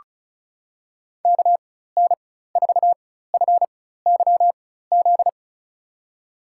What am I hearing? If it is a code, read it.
Morse code, heard as a keyed tone, about 36 wpm, spaced as for 16 wpm, 700 Hz. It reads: KN4FYZ
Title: USA Call Signs - 6 Characters - 4x Character Spacing - 35wpm